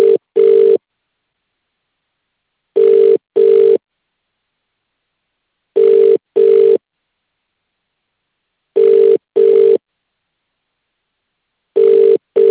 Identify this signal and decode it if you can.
Ringback tone (UK), 400 + 450 Hz.